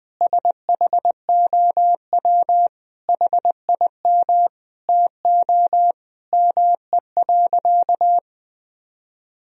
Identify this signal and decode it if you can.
Morse code, heard as a keyed tone, about 21 wpm, 700 Hz. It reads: SHOW HIM TO ME.
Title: Sentences from Top 200 Words and Abbreviations 20wpm